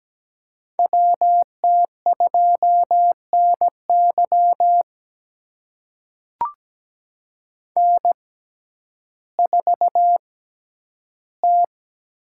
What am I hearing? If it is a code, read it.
Morse code, heard as a keyed tone, about 17 wpm, spaced as for 10 wpm, 700 Hz. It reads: WT2NY N4T
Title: USA Call Signs - 5 Characters - 6x Character Spacing - 17wpm